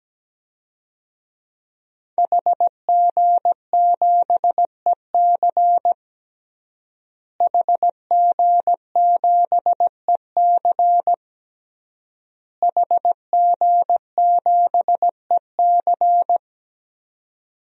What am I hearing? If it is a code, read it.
Morse code, heard as a keyed tone, about 17 wpm, 700 Hz. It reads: HG7EC HG7EC HG7EC
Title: DXCC-100 Call Signs - 5 Characters - Repeated 3x - 17wpm